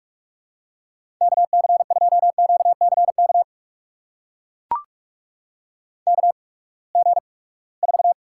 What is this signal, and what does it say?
Morse code, heard as a keyed tone, about 42 wpm, spaced as for 25 wpm, 700 Hz. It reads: KC2QCK KG4
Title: USA Call Signs - 6 Characters - 8x Character Spacing - 45wpm